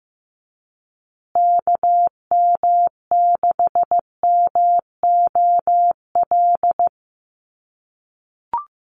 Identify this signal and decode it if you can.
Morse code, heard as a keyed tone, about 15 wpm, 700 Hz. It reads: KM6MOL
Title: USA Call Signs Repeated 2x 15wpm